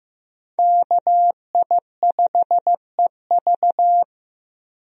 Morse code, heard as a keyed tone, about 15 wpm, 700 Hz. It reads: KI5EV
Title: USA Call Signs - 5 Characters - Repeated 3x - 15wpm